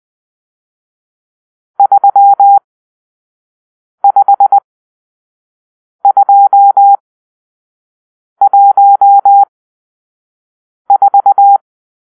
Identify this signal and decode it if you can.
Morse code, heard as a keyed tone, about 20 wpm, spaced as for 5 wpm, 800 Hz. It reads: 35214
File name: arrl / Grp-6 Drill-2